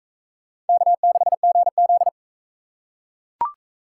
Morse code, heard as a keyed tone, about 41 wpm, 700 Hz. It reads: K6GZ